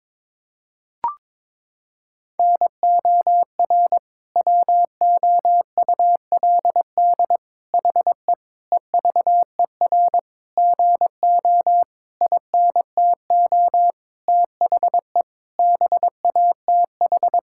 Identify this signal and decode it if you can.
Morse code, heard as a keyed tone, about 21 wpm, 700 Hz. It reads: NOR WOULD HE EVER GO INTO THE BATH
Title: Sentences from Top 1600 Words 22wpm